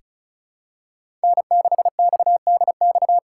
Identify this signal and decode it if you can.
Morse code, heard as a keyed tone, about 36 wpm, 700 Hz. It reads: N6XDX
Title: USA Call Signs - 5 Characters - 3x Character Spacing - 35wpm